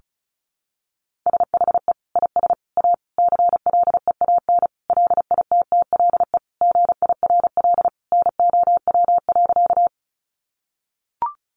Morse code, heard as a keyed tone, about 34 wpm, 700 Hz. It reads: SHE IS A CLEAN LITTLE GIRL NOW.
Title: Sentences from Top 1100 Words 35wpm